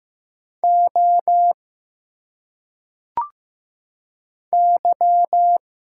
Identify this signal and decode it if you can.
Morse code, heard as a keyed tone, about 15 wpm, 700 Hz. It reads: O Y
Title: Single Letters - Phonetic Pronunciation 15wpm